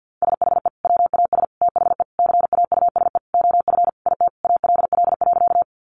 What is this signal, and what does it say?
Morse code, heard as a keyed tone, about 50 wpm, 700 Hz. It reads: SHE WAS THE CAUSE OF IT ALL.
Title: Sentences from Top 200 Words 50wpm